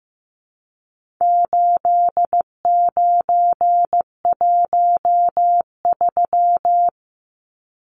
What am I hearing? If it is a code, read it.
Morse code, heard as a keyed tone, about 15 wpm, 700 Hz. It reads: 8913
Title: Numbers - 4-Digit 15wpm